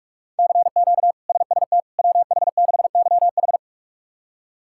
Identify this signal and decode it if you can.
Morse code, heard as a keyed tone, about 46 wpm, 700 Hz. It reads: CQ SST WH6YH
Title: K1USN SST QSO Practice - 45wpm